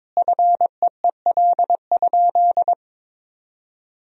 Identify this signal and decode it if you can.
Morse code, heard as a keyed tone, about 22 wpm, 700 Hz. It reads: FEEL?
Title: 2x Word-Spacing - Sentences from Top 400 Words - 22wpm